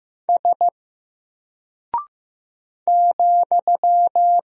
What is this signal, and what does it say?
Morse code, heard as a keyed tone, about 15 wpm, 700 Hz. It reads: S ,
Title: Single Character (Letter:Number:Punctuation:Prosign) 15wpm